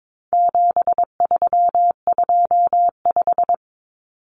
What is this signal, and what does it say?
Morse code, heard as a keyed tone, about 22 wpm, 700 Hz. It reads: 7325
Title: Numbers - 4-Digit 22wpm